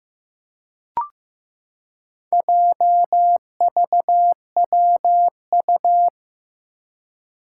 Morse code, heard as a keyed tone, about 15 wpm, 700 Hz. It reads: JVWU
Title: Four Challenging Letter-Number Combinations 15wpm